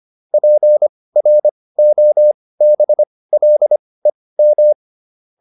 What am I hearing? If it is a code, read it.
Morse code, heard as a keyed tone, about 25 wpm, spaced as for 18 wpm, 600 Hz. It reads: PROBLEM